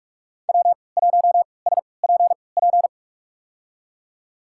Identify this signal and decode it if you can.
Morse code, heard as a keyed tone, about 47 wpm, spaced as for 26 wpm, 700 Hz. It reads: W1SPP
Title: USA Call Signs - 5 Characters - 3x Character Spacing - 45wpm